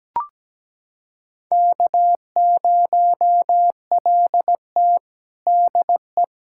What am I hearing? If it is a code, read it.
Morse code, heard as a keyed tone, about 17 wpm, 700 Hz. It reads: K0LT DE
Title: USA Callsign DE Callsign - 4 Characters - 17wpm